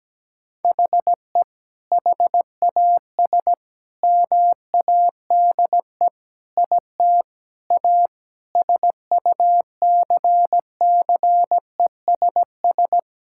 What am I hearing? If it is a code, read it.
Morse code, heard as a keyed tone, about 17 wpm, 700 Hz. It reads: HE HAS MADE IT A SUCCESS
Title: Sentences from Top 1000 Words - Review Part 6 - 17wpm